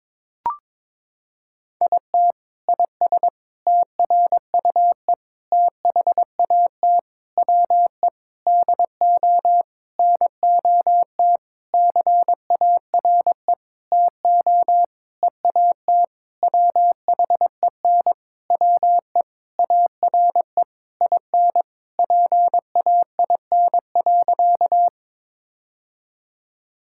Morse code, heard as a keyed tone, about 22 wpm, 700 Hz. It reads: IT IS TRUE THAT WE DO NOT CARE TO EAT WHEN WE ARE IN PAIN.
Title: Sentences from Top 1600 Words 22wpm